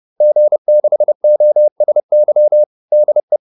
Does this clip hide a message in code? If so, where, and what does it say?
Yes: Morse code from 0.2 to 3.4 s, heard as a keyed tone, about 30 wpm, 600 Hz. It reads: G6OSY DE